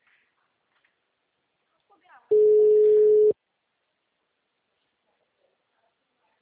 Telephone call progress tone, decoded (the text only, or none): ringback tone